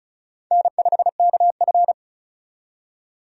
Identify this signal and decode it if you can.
Morse code, heard as a keyed tone, about 34 wpm, 700 Hz. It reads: N5KF
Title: USA Call Signs - 4 Characters - 35wpm